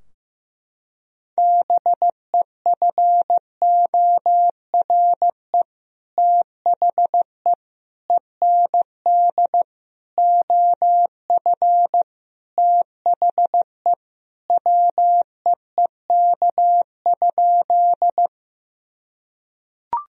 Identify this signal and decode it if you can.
Morse code, heard as a keyed tone, about 15 wpm, 700 Hz. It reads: BEFORE THE END OF THE WEEK?
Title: Sentences from Top 1000 Words - Review Part 2 - 15wpm